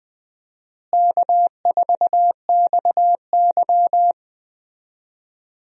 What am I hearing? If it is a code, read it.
Morse code, heard as a keyed tone, about 20 wpm, 700 Hz. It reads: K4XY
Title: USA Call Signs - 4 Characters - Repeated 3x - 20wpm